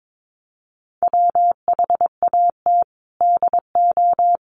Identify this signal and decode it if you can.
Morse code, heard as a keyed tone, about 22 wpm, 700 Hz. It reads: WHAT DO
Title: Sentences from Top 100 Words 22wpm